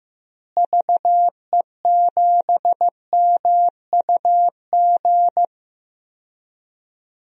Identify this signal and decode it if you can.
Morse code, heard as a keyed tone, about 15 wpm, 700 Hz. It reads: VE7MUG
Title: DXCC-100 Call Signs - 6 Characters - Repeated 3x - 15wpm